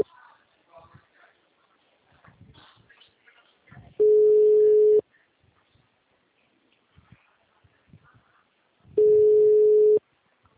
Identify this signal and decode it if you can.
Ringback tone (Europe, 425 Hz), with speech.
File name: out- -2019- - - .232